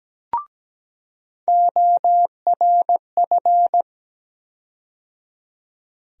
Morse code, heard as a keyed tone, about 17 wpm, 700 Hz. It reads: ORF
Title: USA Call Sign Postfixes - 3 Letters 17wpm